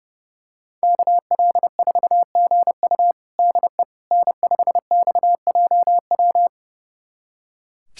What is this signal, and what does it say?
Morse code, heard as a keyed tone, about 30 wpm, 700 Hz. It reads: KL4GU DE N5XJW
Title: USA Callsign DE Callsign - 5 Characters - 30wpm